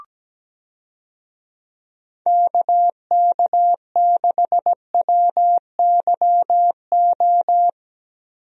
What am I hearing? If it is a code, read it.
Morse code, heard as a keyed tone, about 17 wpm, 700 Hz. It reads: KK6WYO
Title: USA Call Signs - 6 Characters - Repeated 2x - 17wpm